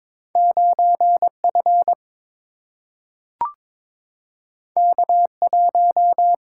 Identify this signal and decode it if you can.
Morse code, heard as a keyed tone, about 21 wpm, 700 Hz. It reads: 9 F K 1
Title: Single-Letter Single-Number 22wpm